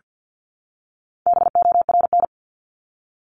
Morse code, heard as a keyed tone, about 48 wpm, 700 Hz. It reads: BORN
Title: Top 751-1000 Words 50wpm